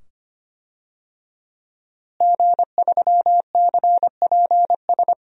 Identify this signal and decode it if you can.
Morse code, heard as a keyed tone, about 25 wpm, 700 Hz. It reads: G3CPS
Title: DXCC-100 Call Signs - 5 Characters - Repeated 2x - 25wpm